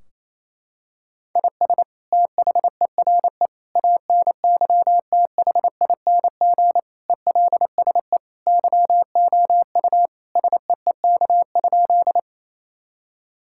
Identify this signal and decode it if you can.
Morse code, heard as a keyed tone, about 28 wpm, 700 Hz. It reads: IS THERE ANYTHING ELSE YOU SEEK?
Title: Sentences from Top 1600 Words 28wpm